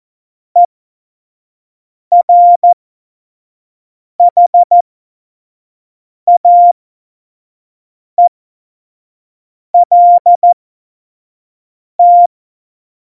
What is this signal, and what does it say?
Morse code, heard as a keyed tone, about 14 wpm, spaced as for 5 wpm, 700 Hz. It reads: ERHAELT
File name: Morsekurs / Witze / Witz 29. Januar 2019, 25 F70